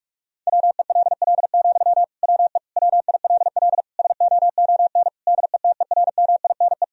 Morse code, heard as a keyed tone, about 46 wpm, 700 Hz. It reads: WELL, WE WILL SOON DETERMINE
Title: Sentences from Top 1000 Words - Review Part 4 - 45wpm